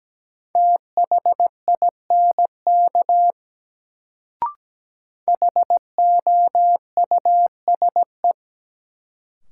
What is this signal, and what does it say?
Morse code, heard as a keyed tone, about 17 wpm, 700 Hz. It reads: THINK HOUSE
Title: Sets of 2 Words - New Word Drill - 17wpm